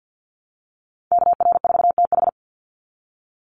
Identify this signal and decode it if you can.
Morse code, heard as a keyed tone, about 50 wpm, 700 Hz. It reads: KR4TH
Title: Call Signs Repeated 3x - 50wpm